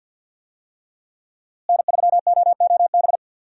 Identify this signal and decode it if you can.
Morse code, heard as a keyed tone, about 51 wpm, 700 Hz. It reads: N3OOB
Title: USA Call Signs - 5 Characters - Repeated 3x - 50wpm